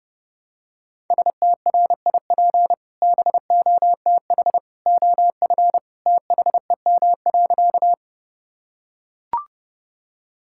Morse code, heard as a keyed tone, about 32 wpm, 700 Hz. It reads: STRIP BOTH OF THEM.
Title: Sentences from Top 1600 Words 30wpm